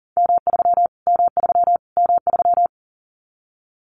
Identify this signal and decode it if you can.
Morse code, heard as a keyed tone, about 40 wpm, 700 Hz. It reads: M3 M3 M3
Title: Callsign Prefixes for Top 25 DXCC - Repeated 3x - 40wpm